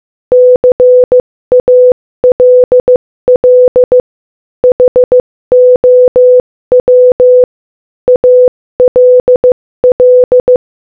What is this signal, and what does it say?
Morse code, heard as a keyed tone, about 15 wpm, 500 Hz. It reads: CALL HOW ALL
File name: Morse-Code-Audio-Speed-15-5